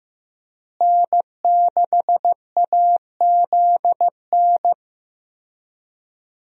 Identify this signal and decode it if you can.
Morse code, heard as a keyed tone, about 15 wpm, 700 Hz. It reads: N6AZN
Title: USA Call Signs - 5 Characters - Repeated 2x - 15wpm